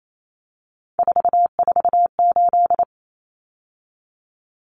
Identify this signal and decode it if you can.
Morse code, heard as a keyed tone, about 28 wpm, 700 Hz. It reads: 448
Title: Common Cut Numbers - 3-Digit 28wpm